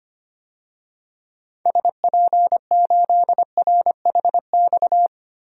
Morse code, heard as a keyed tone, about 25 wpm, 700 Hz. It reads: SP8RHX
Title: DXCC-100 Call Signs - 6 Characters - 25wpm